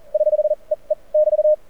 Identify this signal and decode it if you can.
Morse code, heard as a keyed tone, about 42 wpm, spaced as for 30 wpm, 590 Hz. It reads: <HH>EE <BT>
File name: archived / CRSnet